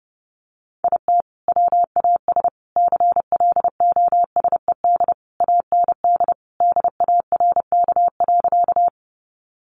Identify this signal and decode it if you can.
Morse code, heard as a keyed tone, about 30 wpm, 700 Hz. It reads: IT WAS CLOSED AND DARK.
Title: Sentences from Top 1400 Words 30wpm